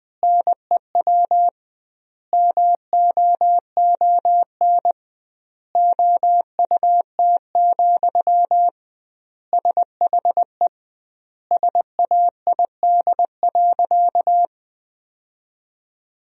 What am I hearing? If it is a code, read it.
Morse code, heard as a keyed tone, about 20 wpm, 700 Hz. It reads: NEW MOON OUT, SHE SAID.
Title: 2x Word-Spacing - Sentences from Top 500 Words - 20wpm